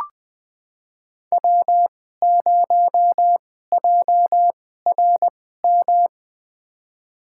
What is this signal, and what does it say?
Morse code, heard as a keyed tone, about 20 wpm, spaced as for 14 wpm, 700 Hz. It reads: W0JRM
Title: USA Call Signs - 5 Characters - 2x Character Spacing - 20wpm